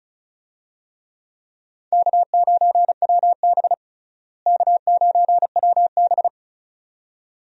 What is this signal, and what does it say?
Morse code, heard as a keyed tone, about 34 wpm, 700 Hz. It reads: K9WB K9WB
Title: USA Call Signs - 4 Characters - Repeated 2x - 35wpm